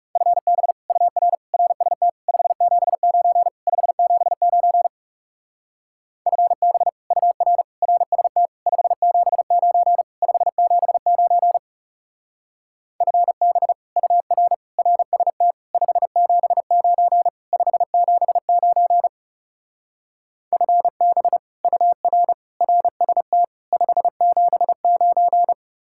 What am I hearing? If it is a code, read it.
Morse code, heard as a keyed tone, about 37 wpm, 700 Hz. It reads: FB UR RST 579 579 FB UR RST 579 579 FB UR RST 579 579 FB UR RST 579
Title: Speed-Racing Common QSO Phrases - 30wpm